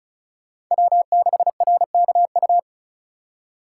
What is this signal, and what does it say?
Morse code, heard as a keyed tone, about 42 wpm, 700 Hz. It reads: W6RKU
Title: Speed Racing - USA 1x3 Call Signs 35wpm